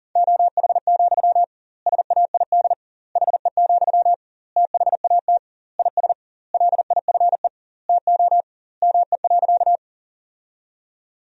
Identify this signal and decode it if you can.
Morse code, heard as a keyed tone, about 40 wpm, 700 Hz. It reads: OH, SAID HE, THAT IS LIFE TO ME.
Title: 2x Word-Spacing - Sentences from Top 500 Words - 40wpm